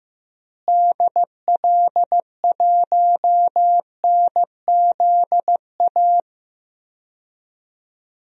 Morse code, heard as a keyed tone, about 15 wpm, 700 Hz. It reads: DL1NZA
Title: DXCC-100 Call Signs - 6 Characters - Repeated 3x - 15wpm